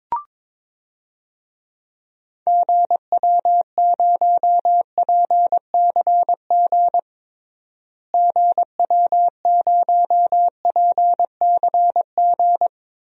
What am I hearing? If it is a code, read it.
Morse code, heard as a keyed tone, about 22 wpm, 700 Hz. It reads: GW0PCG GW0PCG
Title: DXCC-100 Call Signs - 6 Characters - Repeated 3x - 22wpm